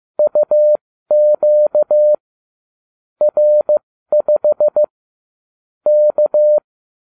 Morse code, heard as a keyed tone, about 15 wpm, spaced as for 13 wpm, 600 Hz. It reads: UQ R5 K